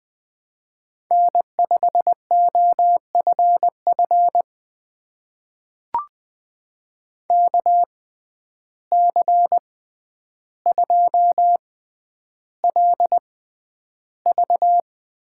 Morse code, heard as a keyed tone, about 20 wpm, spaced as for 10 wpm, 700 Hz. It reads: N5OFF KC2LV
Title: USA Call Signs - 6x Character Spacing - 20wpm